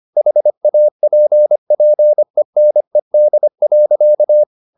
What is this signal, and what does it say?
Morse code, heard as a keyed tone, about 25 wpm, 600 Hz. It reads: HAPPENED.
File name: the old man and the sea part 1 25wpm